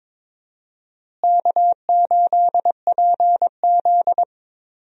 Morse code, heard as a keyed tone, about 21 wpm, 700 Hz. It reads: K8PZ
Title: USA Call Signs - 4 Characters - Repeated 3x - 22wpm